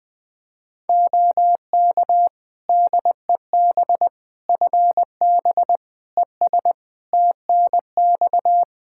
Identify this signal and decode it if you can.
Morse code, heard as a keyed tone, about 20 wpm, 700 Hz. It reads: OK DEB FB ES TNX
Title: Practice QSO - Exchange 2 - Info - 20wpm